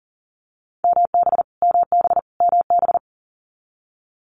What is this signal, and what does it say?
Morse code, heard as a keyed tone, about 40 wpm, 700 Hz. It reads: MB MB MB
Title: Canadian Provinces and Territories - Repeated 3x - 40wpm